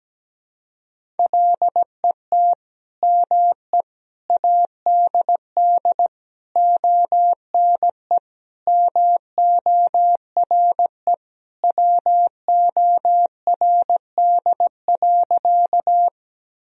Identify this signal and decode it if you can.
Morse code, heard as a keyed tone, about 17 wpm, 700 Hz. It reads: LET ME ADD ONE MORE WORD.
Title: Sentences from Top 1000 Words - Review Part 1 - 17wpm